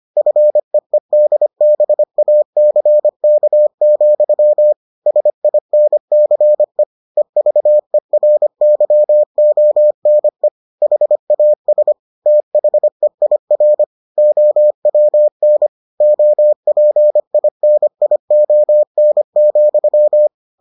Morse code, heard as a keyed tone, about 25 wpm, 600 Hz. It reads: FEEDBACK, SINCE EVERYONE HAS THEIR OWN OPINION,